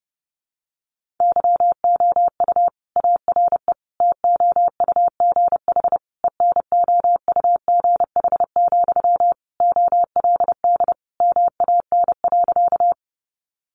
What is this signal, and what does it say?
Morse code, heard as a keyed tone, about 30 wpm, 700 Hz. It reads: YOU ARE TOUGH ENOUGH, OLD MAN.
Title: Sentences from Top 1200 Words 30wpm